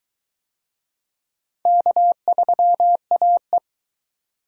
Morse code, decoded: K3AE

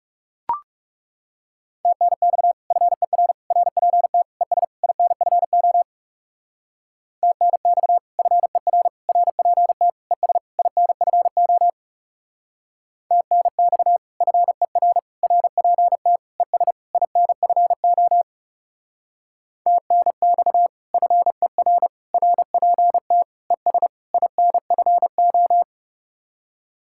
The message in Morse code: TNX FER RPT ES INFO TNX FER RPT ES INFO TNX FER RPT ES INFO TNX FER RPT ES INFO

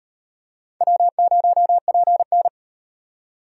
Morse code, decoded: W0PN

